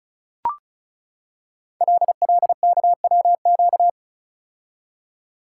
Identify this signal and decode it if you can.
Morse code, heard as a keyed tone, about 35 wpm, 690 Hz. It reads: LLKWQ
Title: Code Groups - 5 Letter:Numbers - 35wpm